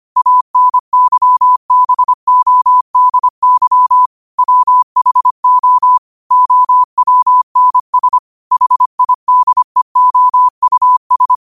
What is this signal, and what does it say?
Morse code, heard as a keyed tone, about 25 wpm, 1000 Hz. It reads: ANYBODY WHO OWNS HIDEOUS
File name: qotdfiles / 25wpm